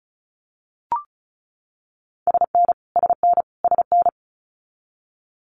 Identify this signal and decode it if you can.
Morse code, heard as a keyed tone, about 36 wpm, 700 Hz. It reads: SN SN SN
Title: Multi-letter Phonemes Repeated 3x 35wpm